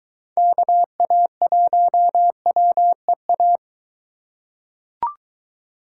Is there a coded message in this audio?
Yes: Morse code from 0.4 to 3.6 s, heard as a keyed tone, about 23 wpm, 700 Hz. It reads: KA1WEA